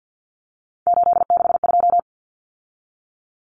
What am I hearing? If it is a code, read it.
Morse code, heard as a keyed tone, about 50 wpm, 700 Hz. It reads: 862
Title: Numbers - 3-Digit 50wpm